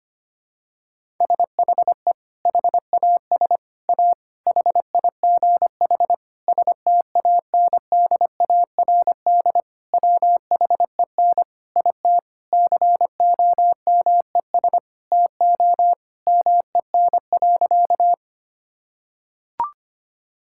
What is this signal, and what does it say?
Morse code, heard as a keyed tone, about 25 wpm, 700 Hz. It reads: SHE HAS A HIGH STANDARD WHEN IT COMES TO MEN.
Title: Sentences from Top 1000 Words - Review Part 6 - 25wpm